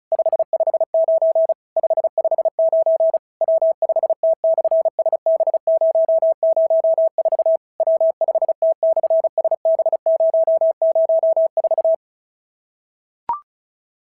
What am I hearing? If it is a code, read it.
Morse code, heard as a keyed tone, about 36 wpm, 650 Hz. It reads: 559 559 W5T/SB004 W5T/SB004
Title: SOTA QSO Practice 35wpm